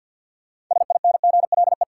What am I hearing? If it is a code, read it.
Morse code, heard as a keyed tone, about 53 wpm, 700 Hz. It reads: SINGLE
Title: Sentences from Top 700 Words - New Word Drill - 50wpm